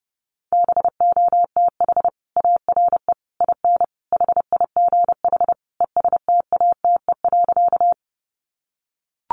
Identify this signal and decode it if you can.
Morse code, heard as a keyed tone, about 30 wpm, 700 Hz. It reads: BOTH ARE IN HIGH ESTATE.
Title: Sentences from Top 1200 Words 30wpm